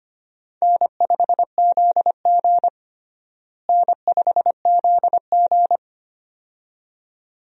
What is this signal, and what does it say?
Morse code, heard as a keyed tone, about 25 wpm, 700 Hz. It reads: N5ZG N5ZG
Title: USA Call Signs - 4 Characters - Repeated 2x - 25wpm